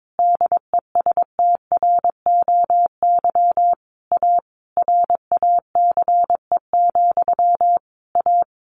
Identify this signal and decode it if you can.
Morse code, heard as a keyed tone, about 22 wpm, 700 Hz. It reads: DESTROY A RACE, A
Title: A Sound Of Thunder 22wpm